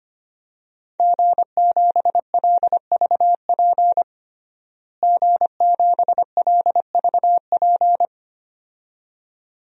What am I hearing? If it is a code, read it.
Morse code, heard as a keyed tone, about 25 wpm, 700 Hz. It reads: G7LVP G7LVP
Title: DXCC-100 Call Signs - 5 Characters - Repeated 2x - 25wpm